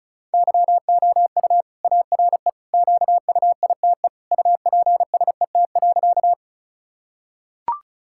Morse code, heard as a keyed tone, about 36 wpm, 700 Hz. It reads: YOU ARE QUITE UPSET.
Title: Sentences from Top 1000 Words - Review Part 5 - 35wpm